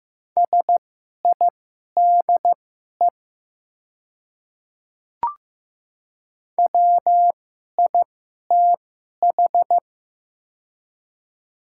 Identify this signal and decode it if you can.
Morse code, heard as a keyed tone, about 15 wpm, spaced as for 11 wpm, 700 Hz. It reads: SIDE WITH
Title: 2x Character Spacing - Top 100 Words - 15wpm